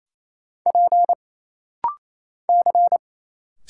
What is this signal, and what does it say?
Morse code, heard as a keyed tone, about 28 wpm, 700 Hz. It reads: P C
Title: Single Letter-Number Alt - Rapid-Fire 28wpm